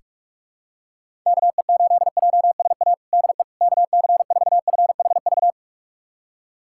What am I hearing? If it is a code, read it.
Morse code, heard as a keyed tone, about 46 wpm, 700 Hz. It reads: KE8JSA DE KC4FHV
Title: USA Callsign DE Callsign - 6 Characters - 45wpm